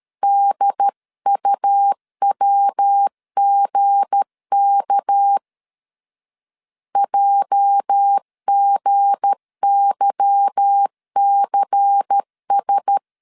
Morse code, heard as a keyed tone, about 13 wpm, 800 Hz. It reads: DUWGK JGYCS
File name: CW / DET / 1 9 1500C S A 12wpm 3800